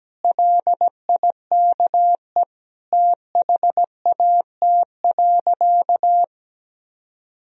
Morse code, heard as a keyed tone, about 17 wpm, 700 Hz. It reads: LIKE THAT.
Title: Sentences from Top 1000 Words - Review Part 2 - 17wpm